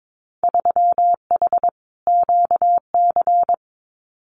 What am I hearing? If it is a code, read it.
Morse code, heard as a keyed tone, about 22 wpm, 700 Hz. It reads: 3H QC
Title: Winter Field Day Exchanges 22wpm